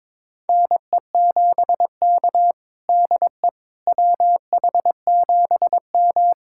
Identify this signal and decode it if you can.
Morse code, heard as a keyed tone, about 22 wpm, 700 Hz. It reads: NE7K DE WH7M
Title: USA Callsign DE Callsign - 4 Characters - 22wpm